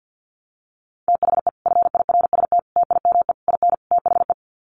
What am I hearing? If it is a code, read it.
Morse code, heard as a keyed tone, about 50 wpm, 700 Hz. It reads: THE FIRST TIME IN THE
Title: Sets of 5 Words Encore 50wpm